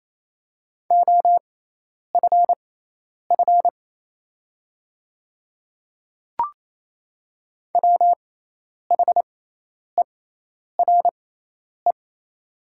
Morse code, heard as a keyed tone, about 28 wpm, spaced as for 10 wpm, 700 Hz. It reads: OFF WHERE
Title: 6x Character Spacing - Top 200 Words - 28wpm